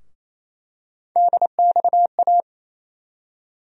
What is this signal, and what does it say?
Morse code, heard as a keyed tone, about 27 wpm, 700 Hz. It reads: DXA
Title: USA Call Sign Postfixes - 3 Letters 28wpm